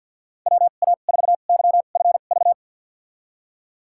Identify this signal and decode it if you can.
Morse code, heard as a keyed tone, about 52 wpm, spaced as for 37 wpm, 700 Hz. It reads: WA4YFV
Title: USA Call Signs - 2x Character Spacing - 50wpm